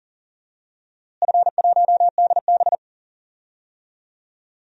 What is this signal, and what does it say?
Morse code, heard as a keyed tone, about 40 wpm, 700 Hz. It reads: F1DB
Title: DXCC-100 Call Signs - 4 Characters - Repeated 2x - 40wpm